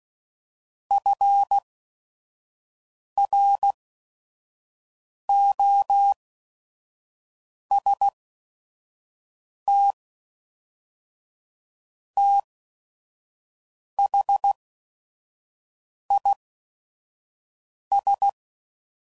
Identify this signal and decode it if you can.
Morse code, heard as a keyed tone, about 16 wpm, spaced as for 5 wpm, 800 Hz. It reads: FROST THIS
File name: CW Code Groups @5WPM / Weather Sets / Weather 4 28